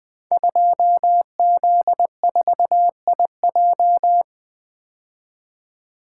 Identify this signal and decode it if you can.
Morse code, heard as a keyed tone, about 20 wpm, 700 Hz. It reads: 2Z4IJ
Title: Code Groups - 5 Letter:Numbers - 20wpm